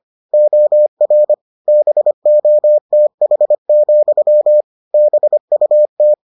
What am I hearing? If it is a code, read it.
Morse code, heard as a keyed tone, about 25 wpm, 600 Hz. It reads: OR BOTH, BUT